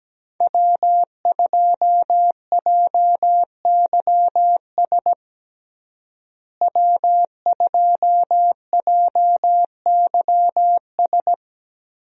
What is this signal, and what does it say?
Morse code, heard as a keyed tone, about 17 wpm, 700 Hz. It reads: W2JYS W2JYS
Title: USA Call Signs - 5 Characters - Repeated 2x - 17wpm